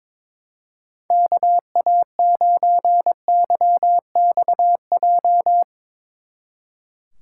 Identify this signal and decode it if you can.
Morse code, heard as a keyed tone, about 22 wpm, 700 Hz. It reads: KA9YXJ
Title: Speed Racing - USA 2x3 Call Signs 22wpm